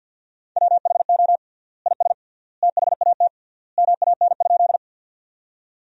Morse code, heard as a keyed tone, about 50 wpm, 700 Hz. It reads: WHO IS THAT MAN?
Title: 3x Word-Spacing - Sentences from Top 200 Words - 50wpm